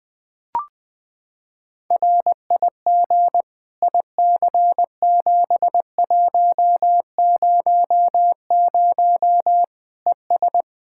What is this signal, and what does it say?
Morse code, heard as a keyed tone, about 20 wpm, 700 Hz. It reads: RIG IC7100 ES
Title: K1USN SST QSO Practice 20wpm